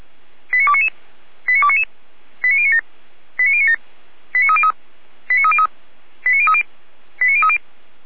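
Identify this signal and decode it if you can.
Five-tone selective call (ZVEI), heard as quick runs of five tones, tones of about 80 ms. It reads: 89299 89299 89097 89097 89303 89303 89030 89030
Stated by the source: ZVEI five-tone selective call received by SDR, from feuerwehr-orlamuende.org fire brigade server.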